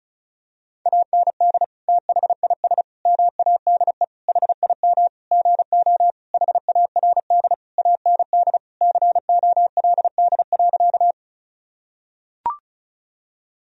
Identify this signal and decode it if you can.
Morse code, heard as a keyed tone, about 36 wpm, 700 Hz. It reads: AND THIS MADE HIM GO HARD AND COLD.
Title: Sentences from Top 1000 Words - Review Part 1 - 35wpm